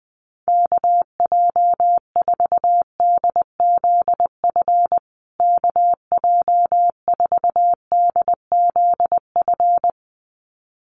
Morse code, heard as a keyed tone, about 20 wpm, 700 Hz. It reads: KJ4DZF KJ4DZF
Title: USA Call Signs Repeated 2x 20wpm